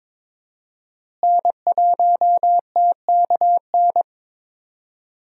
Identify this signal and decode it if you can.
Morse code, heard as a keyed tone, about 21 wpm, 700 Hz. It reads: N1TKN
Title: USA Call Signs - 5 Characters - Repeated 2x - 22wpm